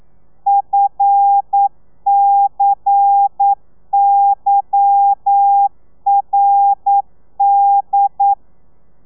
Morse code, heard as a keyed tone, about 9 wpm, 800 Hz. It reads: FCYRD